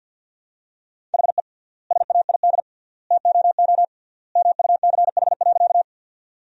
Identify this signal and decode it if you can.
Morse code, heard as a keyed tone, about 53 wpm, 700 Hz. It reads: HE SAID TOO MUCH.
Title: 3x Word-Spacing - Sentences from Top 200 Words - 50wpm